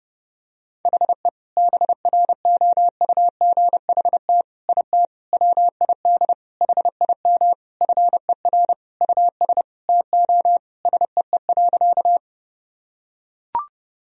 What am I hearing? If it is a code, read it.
Morse code, heard as a keyed tone, about 32 wpm, 700 Hz. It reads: HE BROUGHT IT WID HIM FER US TO SEE.
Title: Sentences from Top 500 Words and Abbreviations 30wpm